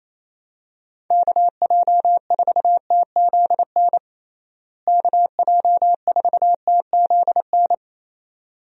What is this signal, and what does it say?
Morse code, heard as a keyed tone, about 28 wpm, 700 Hz. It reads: KJ4TZN KJ4TZN
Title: USA Call Signs - 6 Characters - Repeated 2x - 28wpm